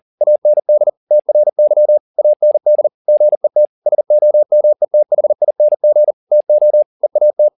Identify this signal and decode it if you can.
Morse code, heard as a keyed tone, about 40 wpm, 600 Hz. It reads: AND TRY AND GET SOMETHING TO EAT
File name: the old man and the sea 2 40wpm